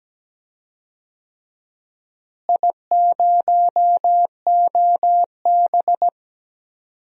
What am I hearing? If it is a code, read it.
Morse code, heard as a keyed tone, about 17 wpm, 700 Hz. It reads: I0OB
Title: DXCC-100 Call Signs - 4 Characters - Repeated 3x - 17wpm